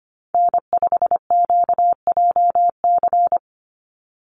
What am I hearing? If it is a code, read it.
Morse code, heard as a keyed tone, about 25 wpm, 700 Hz. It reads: N5QJC
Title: USA Call Signs 25wpm